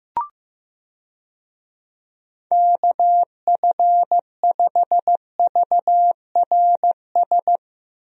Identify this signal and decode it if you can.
Morse code, heard as a keyed tone, about 16 wpm, 700 Hz. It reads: KF5VRS
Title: USA Call Signs - 6 Characters - Repeated 2x - 15wpm